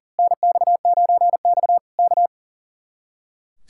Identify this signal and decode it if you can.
Morse code, heard as a keyed tone, about 38 wpm, 700 Hz. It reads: NX9X K
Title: Speed-Racing Common QSO Phrases - 40wpm